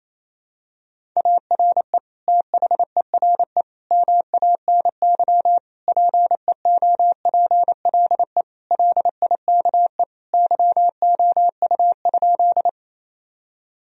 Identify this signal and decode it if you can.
Morse code, heard as a keyed tone, about 28 wpm, 700 Hz. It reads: ARE THERE MANY PEOPLE LIKE YOU?
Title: Sentences from Top 1000 Words - Review Part 1 - 28wpm